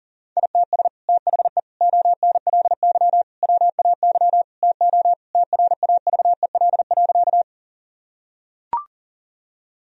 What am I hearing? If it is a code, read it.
Morse code, heard as a keyed tone, about 40 wpm, 700 Hz. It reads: ITS THE ONLY WAY TO TRAVEL.
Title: Sentences from Top 1000 Words - Review Part 2 - 40wpm